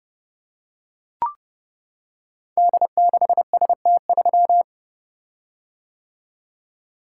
Morse code, heard as a keyed tone, about 30 wpm, 700 Hz. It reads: D6ST3
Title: Code Groups - 5 Letter:Numbers - 30wpm